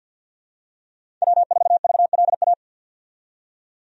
Morse code, heard as a keyed tone, about 53 wpm, 700 Hz. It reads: W4VLA
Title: USA Call Signs - 5 Characters - 50wpm